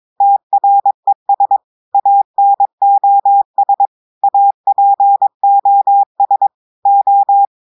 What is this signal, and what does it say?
Morse code, heard as a keyed tone, about 22 wpm, 800 Hz. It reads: TRES ANOS APOS O
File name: audios / portugais / portugais texte en clair 1 vitesse 22 mots minutes